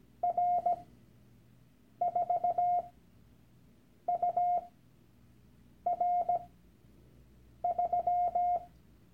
Morse code, heard as a keyed tone, about 17 wpm, spaced as for 6 wpm, 710 Hz. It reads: R4UR3
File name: Workshop Dateien / CW-Online-Kurs / Lektion 001